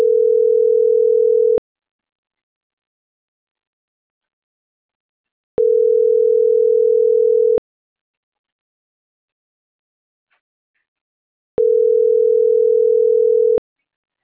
Ringback tone (North America), 440 + 480 Hz.